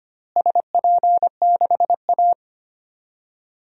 Morse code, heard as a keyed tone, about 25 wpm, 700 Hz. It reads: SP6A
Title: DXCC-100 Call Signs - 4 Characters - Repeated 2x - 25wpm